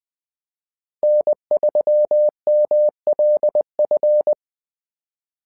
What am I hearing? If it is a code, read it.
Morse code, heard as a keyed tone, about 20 wpm, 600 Hz. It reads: N3MLF
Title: SOTA QSO Practice 20wpm